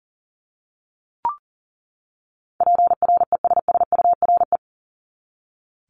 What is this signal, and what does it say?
Morse code, heard as a keyed tone, about 40 wpm, 700 Hz. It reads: PRESSURE
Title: Top 751-1000 Words 40wpm